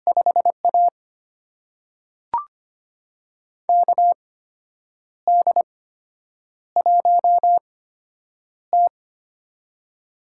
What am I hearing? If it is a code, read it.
Morse code, heard as a keyed tone, about 25 wpm, spaced as for 8 wpm, 700 Hz. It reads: 5A KD1T
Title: USA Call Signs - 4 Characters - 8x Character Spacing - 25wpm